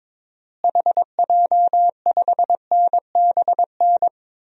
Morse code, heard as a keyed tone, about 22 wpm, 700 Hz. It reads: HJ5NBN
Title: DXCC-100 Call Signs - 6 Characters - Repeated 2x - 22wpm